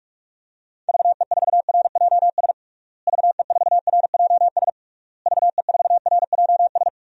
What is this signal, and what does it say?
Morse code, heard as a keyed tone, about 41 wpm, 700 Hz. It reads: VE4RJS VE4RJS VE4RJS
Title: DXCC-100 Call Signs - 6 Characters - Repeated 3x - 45wpm